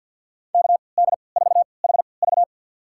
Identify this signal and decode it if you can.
Morse code, heard as a keyed tone, about 52 wpm, spaced as for 28 wpm, 700 Hz. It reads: KD4HV
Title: USA Call Signs - 5 Characters - 3x Character Spacing - 50wpm